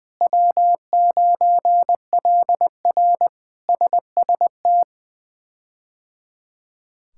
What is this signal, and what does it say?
Morse code, heard as a keyed tone, about 20 wpm, 700 Hz. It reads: W9LR SST
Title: K1USN SST QSO Practice - 20wpm